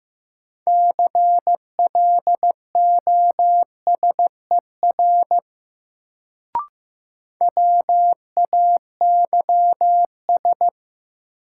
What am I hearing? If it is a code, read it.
Morse code, heard as a keyed tone, about 15 wpm, 700 Hz. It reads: CLOSER WAYS
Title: Sets of 5 Words Encore - New Word Drill - 15wpm